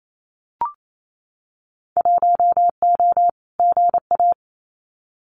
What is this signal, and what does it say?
Morse code, heard as a keyed tone, about 28 wpm, 700 Hz. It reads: 1O GA
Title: Winter Field Day Exchanges 28wpm